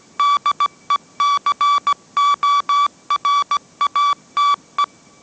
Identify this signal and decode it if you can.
Morse code, heard as a keyed tone, about 17 wpm, 1150 Hz. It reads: DECORATE
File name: mrx / mrx080 18wpm